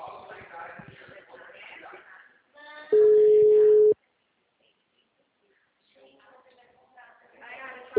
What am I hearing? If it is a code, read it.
Ringback tone (Europe, 425 Hz), with speech.